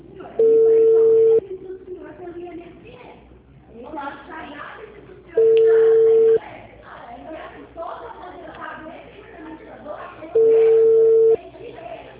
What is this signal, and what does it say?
Ringback tone (China, 450 Hz), with speech.